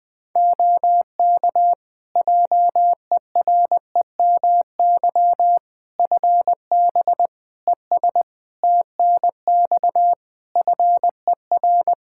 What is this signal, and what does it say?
Morse code, heard as a keyed tone, about 21 wpm, 700 Hz. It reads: OK JEREMY FB ES TNX FER
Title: Practice QSO - Exchange 2 - Info - 20wpm